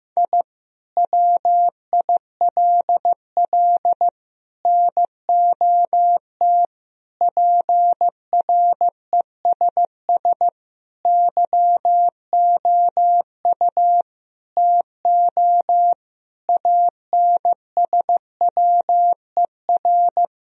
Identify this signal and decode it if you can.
Morse code, heard as a keyed tone, about 15 wpm, 700 Hz. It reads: I WILL NOT PRESS YOU TO ANSWER
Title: Sentences from Top 1000 Words - Review Part 1 - 15wpm